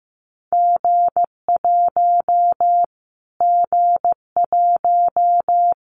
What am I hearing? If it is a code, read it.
Morse code, heard as a keyed tone, about 15 wpm, 700 Hz. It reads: G1 G1
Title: Callsign Prefixes for Top 25 DXCC - Repeated 3x - 15wpm